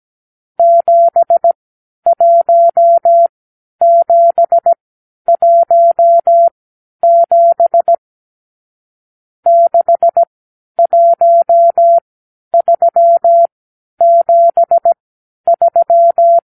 Morse code, heard as a keyed tone, about 17 wpm, spaced as for 10 wpm, 680 Hz. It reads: 71717 61373
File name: k7qo / Lesson 52 (8)